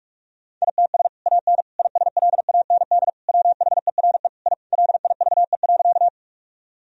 Morse code, heard as a keyed tone, about 46 wpm, 700 Hz. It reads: ITS AN ISLAND WHERE I LIVE.
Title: Sentences from Top 1000 Words - Review Part 3 - 45wpm